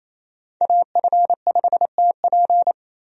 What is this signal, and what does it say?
Morse code, heard as a keyed tone, about 28 wpm, 700 Hz. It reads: AF5TP
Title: USA Call Signs - 5 Characters - 28wpm